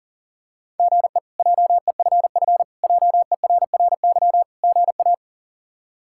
Morse code, heard as a keyed tone, about 40 wpm, 700 Hz. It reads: GE JEFF JERRY GA
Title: K1USN SST QSO Practice - 40wpm